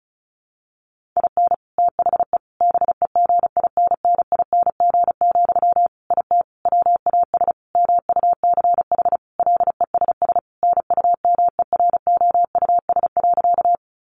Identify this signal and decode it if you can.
Morse code, heard as a keyed tone, about 34 wpm, 700 Hz. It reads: IN THE BEGINNING, IT WAS MUCH LESS NUMEROUS.
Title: Sentences from Top 1400 Words 35wpm